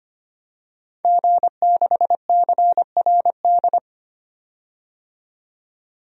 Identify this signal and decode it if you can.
Morse code, heard as a keyed tone, about 25 wpm, 700 Hz. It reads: G6CRD
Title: DXCC-100 Call Signs - 5 Characters - 25wpm